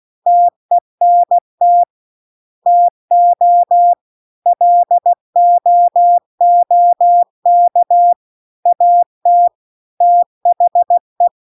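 Morse code, heard as a keyed tone, about 16 wpm, 700 Hz. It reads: TENT TO LOOK AT THE
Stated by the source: Morse practice text at 16 WPM.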